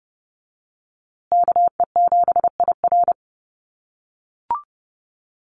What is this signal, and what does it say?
Morse code, heard as a keyed tone, about 30 wpm, 700 Hz. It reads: KE7IR
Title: USA Call Signs Repeated 2x 30wpm